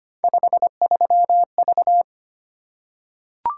Morse code, heard as a keyed tone, about 25 wpm, 700 Hz. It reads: 53V
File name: Three Letter-Number Combinations - 25wpm